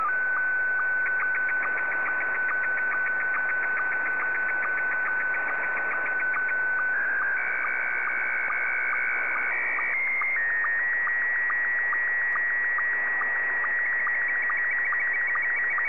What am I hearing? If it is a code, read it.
SSTV (slow-scan television), Scottie 1.